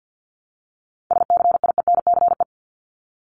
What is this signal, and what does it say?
Morse code, heard as a keyed tone, about 50 wpm, 700 Hz. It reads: SCIENCE
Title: Top 500 Words Repeated 50wpm